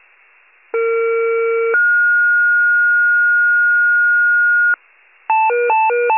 Two-tone page, heard as a one-second tone, then a three-second tone, about 460 then 1430 Hz. It is heard from 0.7 to 4.8 s.
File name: wcalert